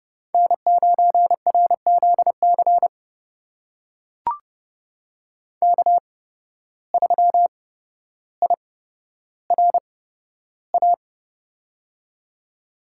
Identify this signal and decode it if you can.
Morse code, heard as a keyed tone, about 30 wpm, spaced as for 13 wpm, 700 Hz. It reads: N9RZC K3IRA
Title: USA Call Signs - 5 Characters - 8x Character Spacing - 30wpm